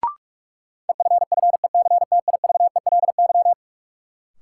Morse code, heard as a keyed tone, about 41 wpm, 700 Hz. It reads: EFFECTIVELY
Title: Sentences from Top 1400 Words - New Word Drill - 45wpm